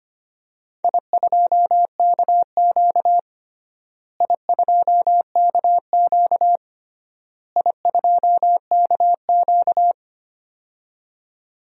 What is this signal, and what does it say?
Morse code, heard as a keyed tone, about 25 wpm, 700 Hz. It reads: I2KQ I2KQ I2KQ
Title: DXCC-100 Call Signs - 4 Characters - Repeated 3x - 25wpm